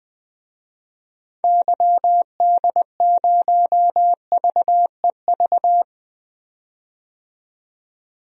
Morse code, heard as a keyed tone, about 21 wpm, 700 Hz. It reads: YD0VEV
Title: DXCC-100 Call Signs - 6 Characters - 20wpm